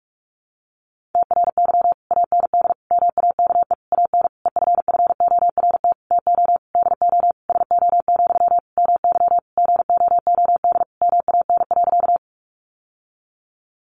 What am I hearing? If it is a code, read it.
Morse code, heard as a keyed tone, about 44 wpm, 700 Hz. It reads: TRY AND MAKE AN EFFORT TO DO SO, MY GOOD MAN.
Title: Sentences from Top 1200 Words 45wpm